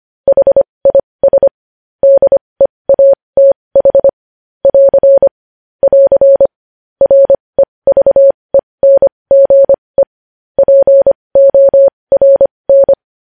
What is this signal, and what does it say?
Morse code, heard as a keyed tone, about 25 wpm, spaced as for 20 wpm, 550 Hz. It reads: HIS DEATH <AR> <AR> REVENGE PORN
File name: podcast / News headlines-1x-25x20